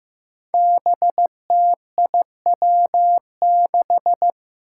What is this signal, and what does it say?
Morse code, heard as a keyed tone, about 15 wpm, 700 Hz. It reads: BTIW6
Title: Five Challenging Letter-Number Combinations 15wpm